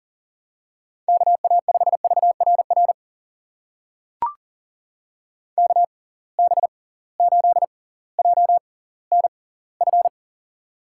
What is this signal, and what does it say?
Morse code, heard as a keyed tone, about 40 wpm, spaced as for 21 wpm, 700 Hz. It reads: KA5VRR KB8JNF
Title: USA Call Signs - 6 Characters - 6x Character Spacing - 40wpm